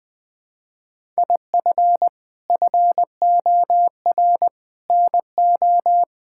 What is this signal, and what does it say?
Morse code, heard as a keyed tone, about 20 wpm, 700 Hz. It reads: IF FOR NO
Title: Sentences with 6-Word Phrases - Spoken Beginning - 20wpm